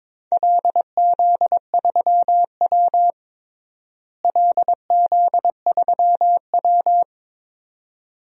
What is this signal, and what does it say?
Morse code, heard as a keyed tone, about 22 wpm, 700 Hz. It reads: LZ3W LZ3W
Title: DXCC-100 Call Signs - 4 Characters - Repeated 2x - 22wpm